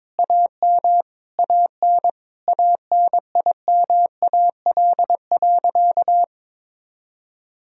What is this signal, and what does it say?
Morse code, heard as a keyed tone, about 22 wpm, 700 Hz. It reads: AM AN ANIMAL.
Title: Sentences from Top 1000 Words - Review Part 1 - 22wpm